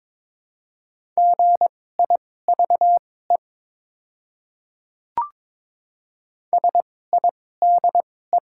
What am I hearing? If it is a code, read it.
Morse code, heard as a keyed tone, about 21 wpm, spaced as for 16 wpm, 700 Hz. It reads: GIVE SIDE
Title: 2x Character Spacing - Top 200 Words - 22wpm